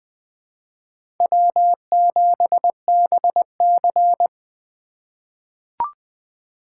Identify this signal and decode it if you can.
Morse code, heard as a keyed tone, about 20 wpm, 700 Hz. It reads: W7BC
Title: USA Call Signs - 4 Characters - 2x Character Spacing - 20wpm